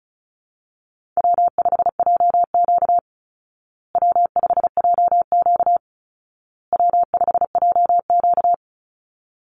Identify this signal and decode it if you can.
Morse code, heard as a keyed tone, about 34 wpm, 700 Hz. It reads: W5JQ W5JQ W5JQ
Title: Call Signs Repeated 2x - 35wpm